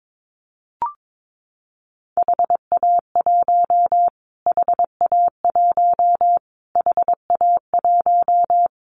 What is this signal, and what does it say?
Morse code, heard as a keyed tone, about 22 wpm, 700 Hz. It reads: HA1 HA1 HA1
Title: Callsign Prefixes for Top 25 DXCC - Repeated 3x - 22wpm